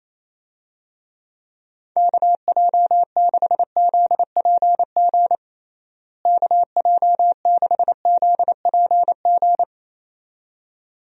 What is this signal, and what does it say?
Morse code, heard as a keyed tone, about 28 wpm, 700 Hz. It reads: KJ6ZPG KJ6ZPG
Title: USA Call Signs - 6 Characters - Repeated 2x - 28wpm